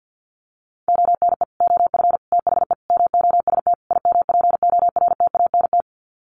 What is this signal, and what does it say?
Morse code, heard as a keyed tone, about 50 wpm, 700 Hz. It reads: ONE OF THE MOST IMPORTANT
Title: Sets of 5 Words 50wpm